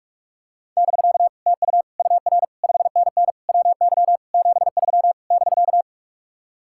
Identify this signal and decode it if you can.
Morse code, heard as a keyed tone, about 49 wpm, 700 Hz. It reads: <BK> TU UR 5NN WY 73 <BK>